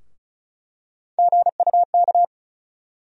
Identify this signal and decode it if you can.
Morse code, heard as a keyed tone, about 34 wpm, 700 Hz. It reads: GUK
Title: USA Call Sign Postfixes - 3 Letters 35wpm